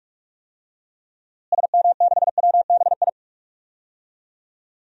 Morse code, heard as a keyed tone, about 46 wpm, 700 Hz. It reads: SM6WBI
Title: DXCC-100 Call Signs - 6 Characters - Repeated 2x - 45wpm